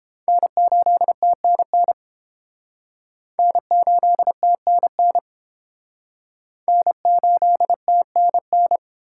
Morse code, heard as a keyed tone, about 29 wpm, 700 Hz. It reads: N8TNN N8TNN N8TNN